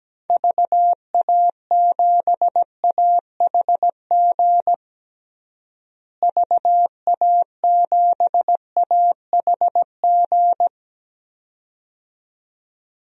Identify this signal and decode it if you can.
Morse code, heard as a keyed tone, about 17 wpm, 700 Hz. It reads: VA7AHG VA7AHG
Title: DXCC-100 Call Signs - 6 Characters - Repeated 2x - 17wpm